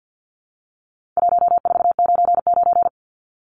Morse code, heard as a keyed tone, about 49 wpm, 700 Hz. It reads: 1499
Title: Numbers - 4-Digit 50wpm